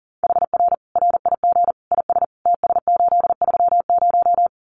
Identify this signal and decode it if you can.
Morse code, heard as a keyed tone, about 40 wpm, 700 Hz. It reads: HR RIG IS TS830
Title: Common QSO Phrases 40wpm